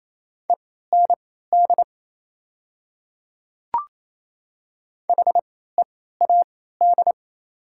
Morse code, heard as a keyed tone, about 27 wpm, spaced as for 16 wpm, 690 Hz. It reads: END HEAD
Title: 3x Character Spacing - Top 500 Words - 28wpm